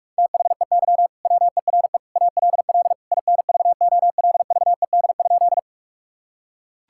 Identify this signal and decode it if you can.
Morse code, heard as a keyed tone, about 41 wpm, 700 Hz. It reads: THEY WERE ALL INVOLVED?
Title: Sentences from Top 1000 Words - Review Part 5 - 45wpm